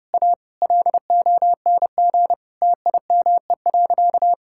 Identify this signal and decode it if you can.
Morse code, heard as a keyed tone, about 32 wpm, 700 Hz. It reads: A LONG TIME.
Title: Sentences from Top 1600 Words 30wpm